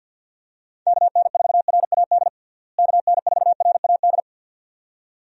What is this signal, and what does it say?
Morse code, heard as a keyed tone, about 48 wpm, 700 Hz. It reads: KN4RAD KN4RAD
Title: USA Call Signs - 6 Characters - Repeated 2x - 50wpm